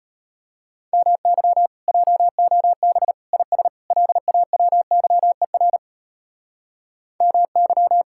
Morse code, heard as a keyed tone, about 30 wpm, 700 Hz. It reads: MY JOB IS LAWYER MY
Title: Speed-Racing Common QSO Phrases Encore - 25wpm